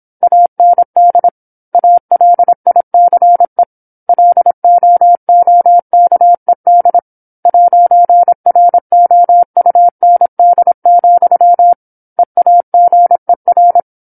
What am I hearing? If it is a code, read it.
Morse code, heard as a keyed tone, about 27 wpm, 700 Hz. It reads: AND ALICE LOOKED 'ROUND, EAGER
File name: 08-Chapter-08 26 WPM